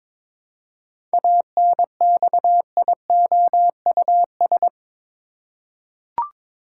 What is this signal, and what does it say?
Morse code, heard as a keyed tone, about 22 wpm, 700 Hz. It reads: ANXIOUS
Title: Seven Letter Words - 22wpm